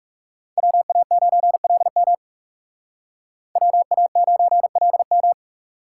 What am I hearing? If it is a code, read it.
Morse code, heard as a keyed tone, about 37 wpm, 700 Hz. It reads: WA9LM WA9LM
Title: Speed Racing - USA Call Signs 30wpm